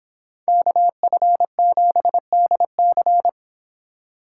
Morse code, heard as a keyed tone, about 29 wpm, 700 Hz. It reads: KF7DC